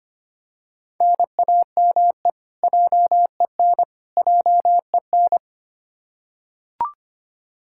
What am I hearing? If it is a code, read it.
Morse code, heard as a keyed tone, about 25 wpm, 700 Hz. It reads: NAME JEN JEN
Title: K1USN SST QSO Practice 25wpm